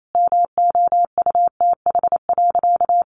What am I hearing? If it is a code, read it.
Morse code, heard as a keyed tone, about 28 wpm, 700 Hz. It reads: MOUTH.
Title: Sentences from Top 1500 Words 28wpm